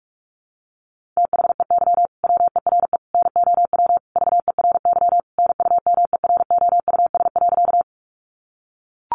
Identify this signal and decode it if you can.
Morse code, heard as a keyed tone, about 44 wpm, 700 Hz. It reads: THEY WERE NOW VERY NUMEROUS.
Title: Sentences from Top 1400 Words 45wpm